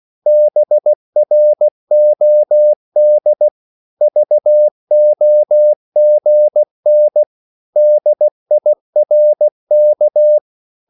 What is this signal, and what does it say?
Morse code, heard as a keyed tone, about 16 wpm, 600 Hz. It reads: BROD VOGN DIRK
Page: Korte danske ord 16wpm